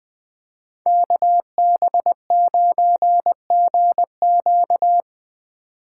Morse code, heard as a keyed tone, about 20 wpm, 700 Hz. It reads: KB9GQ
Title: Speed Racing - USA 2x2 Call Signs 17wpm